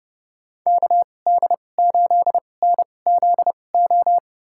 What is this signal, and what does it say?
Morse code, heard as a keyed tone, about 30 wpm, spaced as for 22 wpm, 700 Hz. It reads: KD8NZO
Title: USA Call Signs - 6 Characters - 2x Character Spacing - 30wpm